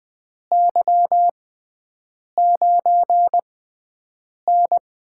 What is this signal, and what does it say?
Morse code, heard as a keyed tone, about 20 wpm, spaced as for 7 wpm, 700 Hz. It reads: Y9N
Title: USA Call Signs - 6x Character Spacing - 20wpm